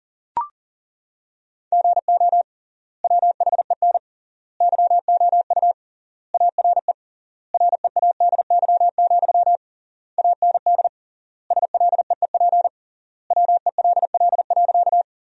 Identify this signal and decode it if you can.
Morse code, heard as a keyed tone, about 37 wpm, 700 Hz. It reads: GO WHEN YOU ARE READY, AND SLEEP WELL.
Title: 3x Word-Spacing - Sentences from Top 500 Words - 40wpm